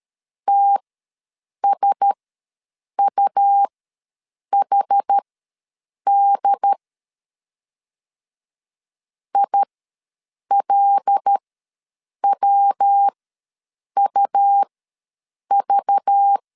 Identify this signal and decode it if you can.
Morse code, heard as a keyed tone, about 13 wpm, spaced as for 7 wpm, 800 Hz. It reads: TSUHD ILWUV
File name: LEZIONI / 5 LBVC / 5 LBV ws 12wpm 3015